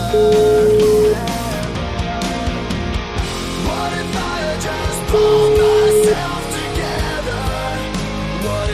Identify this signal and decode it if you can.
Ringback tone (Europe, 425 Hz), with noise.